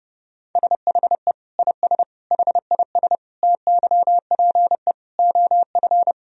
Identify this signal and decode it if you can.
Morse code, heard as a keyed tone, about 30 wpm, 700 Hz. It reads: SHE IS HIS TYPE OF
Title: Sentences from Top 1000 Words - Review Part 3 - 30wpm